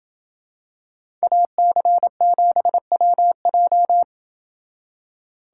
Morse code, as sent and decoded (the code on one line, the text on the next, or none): .- -.-. --... .-- .---
AC7WJ